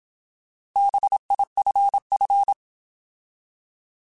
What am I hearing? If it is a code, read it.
Morse code, heard as a keyed tone, about 27 wpm, 800 Hz. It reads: BIFF
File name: Alphabetical / Alphabetical mp3s / Bs @25